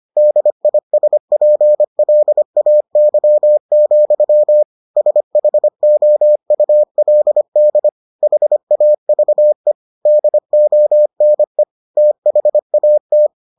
Morse code, heard as a keyed tone, about 25 wpm, 600 Hz. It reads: DISPLAY, SHOULD HAVE DONE THAT